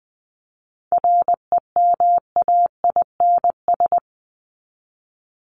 Morse code, heard as a keyed tone, about 20 wpm, 700 Hz. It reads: REMAINS
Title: ES:S Suffix - Top 100 Words with Base Words 20wpm